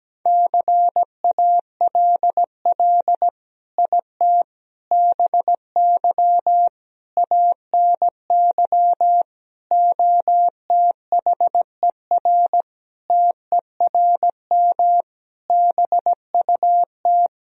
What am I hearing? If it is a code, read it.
Morse code, heard as a keyed tone, about 17 wpm, 700 Hz. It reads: CALL IT BY ANY OTHER TERM BUT
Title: Sentences from Top 1000 Words - Review Part 4 - 17wpm